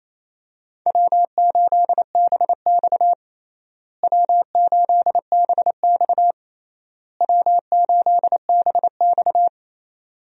Morse code, heard as a keyed tone, about 28 wpm, 700 Hz. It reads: W8BX W8BX W8BX
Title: USA Call Signs - 4 Characters - Repeated 3x - 28wpm